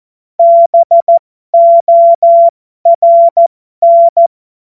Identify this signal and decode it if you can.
Morse code, heard as a keyed tone, about 14 wpm, spaced as for 12 wpm, 680 Hz. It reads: BORN
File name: Morsekurs / Witze